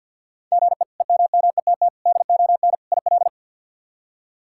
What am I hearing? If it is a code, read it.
Morse code, heard as a keyed tone, about 52 wpm, 700 Hz. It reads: GE EMMETT DON IL
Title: K1USN SST QSO Practice - 50wpm